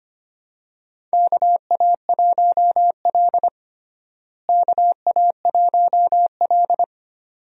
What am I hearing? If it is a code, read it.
Morse code, heard as a keyed tone, about 25 wpm, 700 Hz. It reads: KA1L KA1L
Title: USA Call Signs - 4 Characters - Repeated 2x - 25wpm